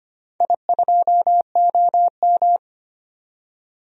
Morse code, heard as a keyed tone, about 25 wpm, 700 Hz. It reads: I2OM
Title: DXCC-100 Call Signs - 4 Characters - 25wpm